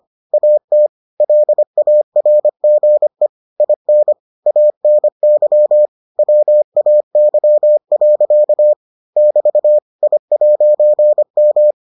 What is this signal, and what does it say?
Morse code, heard as a keyed tone, about 25 wpm, 600 Hz. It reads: AT LARGE IN ANY WAY. <BT> I'M